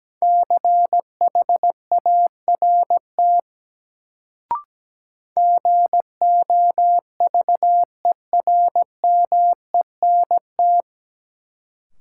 Morse code, heard as a keyed tone, about 17 wpm, 700 Hz. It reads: CHART GOVERMENT
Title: Sentences from Top 1300 Words - New Word Drill Alt - 17wpm